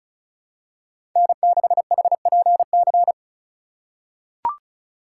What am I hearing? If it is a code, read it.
Morse code, heard as a keyed tone, about 36 wpm, 700 Hz. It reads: N6HPC
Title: USA Call Signs - 5 Characters - 4x Character Spacing - 35wpm